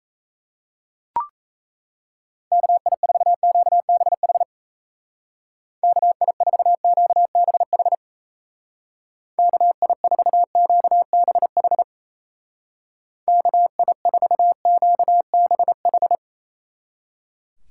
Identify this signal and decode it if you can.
Morse code, heard as a keyed tone, about 37 wpm, 700 Hz. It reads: KI4QBH KI4QBH KI4QBH KI4QBH